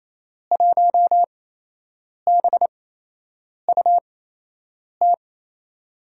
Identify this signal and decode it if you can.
Morse code, heard as a keyed tone, about 27 wpm, spaced as for 8 wpm, 700 Hz. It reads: 1BUT
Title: USA Call Signs - 5 Characters - 8x Character Spacing - 28wpm